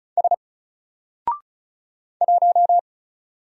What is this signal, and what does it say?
Morse code, heard as a keyed tone, about 33 wpm, 700 Hz. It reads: S 1
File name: Single-Letter-Number-35wpm